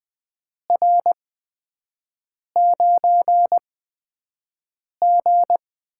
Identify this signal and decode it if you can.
Morse code, heard as a keyed tone, about 20 wpm, spaced as for 5 wpm, 700 Hz. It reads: R9G
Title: USA Call Signs - 8x Character Spacing - 20wpm